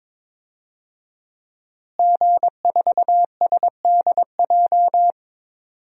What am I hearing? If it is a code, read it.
Morse code, heard as a keyed tone, about 21 wpm, 700 Hz. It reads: G4SDJ
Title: DXCC-100 Call Signs - 5 Characters - Repeated 2x - 22wpm